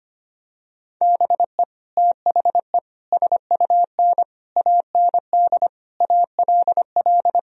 Morse code, heard as a keyed tone, about 25 wpm, 700 Hz. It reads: BE THE SUN AND ALL
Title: Sentences from Top 1000 Words - Review Part 1 - 25wpm